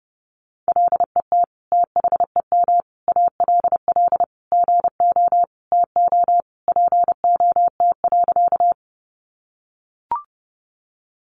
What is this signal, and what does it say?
Morse code, heard as a keyed tone, about 30 wpm, 700 Hz. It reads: LET THEM ALL GO TO POT.
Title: Sentences from Top 1000 Words 30wpm